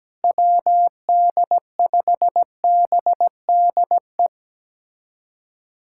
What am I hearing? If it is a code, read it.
Morse code, heard as a keyed tone, about 17 wpm, 700 Hz. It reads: WD5BDE
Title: USA Callsign DE Callsign 17wpm